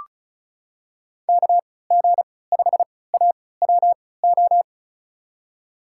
Morse code, heard as a keyed tone, about 36 wpm, spaced as for 20 wpm, 700 Hz. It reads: KG5AWO
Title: USA Call Signs - 6 Characters - 3x Character Spacing - 35wpm